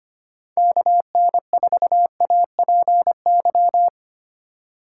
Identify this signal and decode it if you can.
Morse code, heard as a keyed tone, about 25 wpm, 700 Hz. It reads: KN4APY
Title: USA Call Signs - 6 Characters - 6x Character Spacing - 25wpm